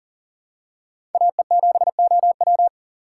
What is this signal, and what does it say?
Morse code, heard as a keyed tone, about 40 wpm, 700 Hz. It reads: AE7OW
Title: USA Call Signs - 5 Characters - 3x Character Spacing - 40wpm